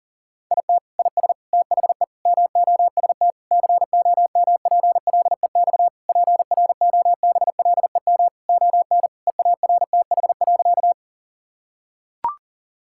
Morse code, heard as a keyed tone, about 37 wpm, 700 Hz. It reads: IT IS THE MOST COMPLEX PROBLEM ON EARTH.
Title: Sentences from Top 1000 Words - Review Part 6 - 40wpm